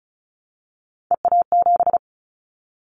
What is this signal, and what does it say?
Morse code, heard as a keyed tone, about 36 wpm, 700 Hz. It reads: EA7
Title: Callsign Prefixes for Top 25 DXCC - 35wpm